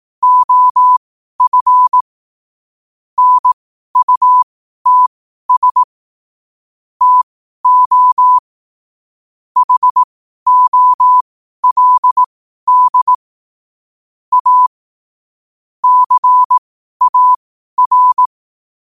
Morse code, heard as a keyed tone, about 18 wpm, spaced as for 12 wpm, 1000 Hz. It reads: OF NUTS TO HOLD A CAR
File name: qotdfiles / 12wpm